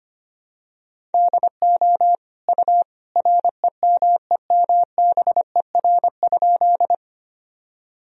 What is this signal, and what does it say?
Morse code, heard as a keyed tone, about 25 wpm, 700 Hz. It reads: DO U REMEMBER?
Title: Sentences from Top 400 Words and Abbreviations 25wpm